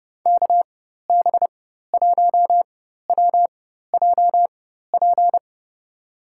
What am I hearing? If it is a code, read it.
Morse code, heard as a keyed tone, about 30 wpm, spaced as for 14 wpm, 700 Hz. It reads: KB1WJP
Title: USA Call Signs - 6 Characters - 4x Character Spacing - 30wpm